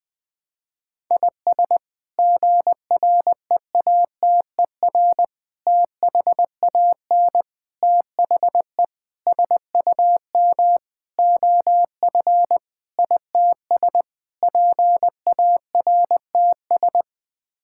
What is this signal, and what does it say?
Morse code, heard as a keyed tone, about 21 wpm, 700 Hz. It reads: IS GREATER THAN THE SUM OF ITS PARTS
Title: Sets of 8 Words - 20wpm